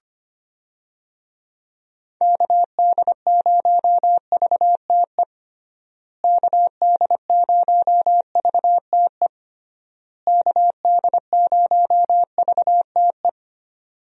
Morse code, heard as a keyed tone, about 25 wpm, 700 Hz. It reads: KD0VTE KD0VTE KD0VTE
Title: USA Call Signs - 6 Characters - Repeated 3x - 25wpm